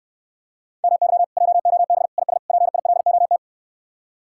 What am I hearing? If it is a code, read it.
Morse code, heard as a keyed tone, about 67 wpm, 700 Hz. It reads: MY JOB IS PILOT